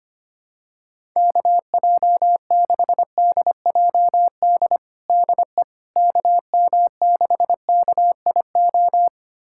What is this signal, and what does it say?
Morse code, heard as a keyed tone, about 25 wpm, 700 Hz. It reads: KJ6DJD DE KM6KIO
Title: USA Callsign DE Callsign 25wpm